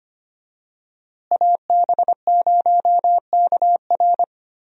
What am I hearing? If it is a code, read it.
Morse code, heard as a keyed tone, about 25 wpm, 700 Hz. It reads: AB0KR
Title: USA Call Signs - 5 Characters - Repeated 2x - 25wpm